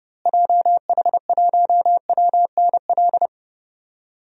Morse code, heard as a keyed tone, about 30 wpm, 700 Hz. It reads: JH1WNL
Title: DXCC-100 Call Signs - 6 Characters - Repeated 3x - 30wpm